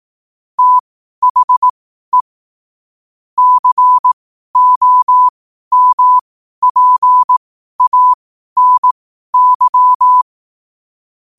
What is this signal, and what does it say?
Morse code, heard as a keyed tone, about 18 wpm, spaced as for 12 wpm, 1000 Hz. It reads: THE COMPANY
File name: qotdfiles / 12wpm